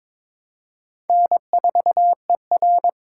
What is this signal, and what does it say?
Morse code, heard as a keyed tone, about 22 wpm, 700 Hz. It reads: N4ER
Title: USA Call Signs - 4 Characters - 4x Character Spacing - 22wpm